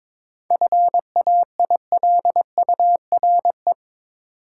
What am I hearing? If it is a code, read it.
Morse code, heard as a keyed tone, about 22 wpm, 700 Hz. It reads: FAILURE
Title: Sentences from Top 900 Words - New Word Drill - 22wpm